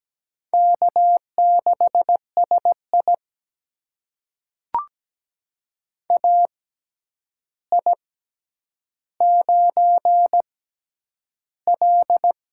Morse code, heard as a keyed tone, about 17 wpm, spaced as for 9 wpm, 700 Hz. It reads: K6SI AI9L
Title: USA Call Signs - 4 Characters - 6x Character Spacing - 17wpm